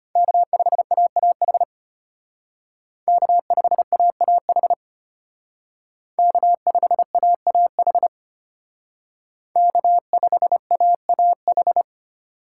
Morse code, heard as a keyed tone, about 29 wpm, 700 Hz. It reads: K5AAH K5AAH K5AAH K5AAH
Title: Speed Racing - USA 1x3 Call Signs 25wpm